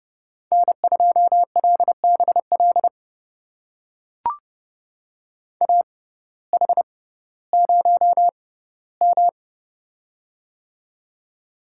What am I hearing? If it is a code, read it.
Morse code, heard as a keyed tone, about 30 wpm, spaced as for 16 wpm, 700 Hz. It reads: N2LBL AH0M
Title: USA Call Signs - 6x Character Spacing - 30wpm